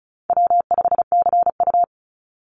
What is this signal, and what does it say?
Morse code, heard as a keyed tone, about 34 wpm, 700 Hz. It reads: W5CU
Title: Easy Call Signs Repeated 3x - 35wpm